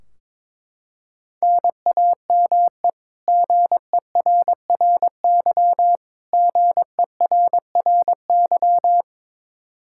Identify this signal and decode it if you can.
Morse code, heard as a keyed tone, about 22 wpm, 700 Hz. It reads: NAME GERRY GERRY
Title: Practice QSO - Exchange 1 - Beginning - 22wpm